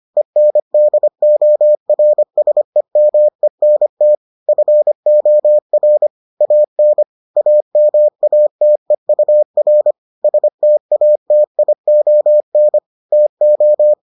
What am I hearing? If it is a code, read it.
Morse code, heard as a keyed tone, about 25 wpm, 600 Hz. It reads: ENDORSEMENT FOR AN AMATEUR STATION TO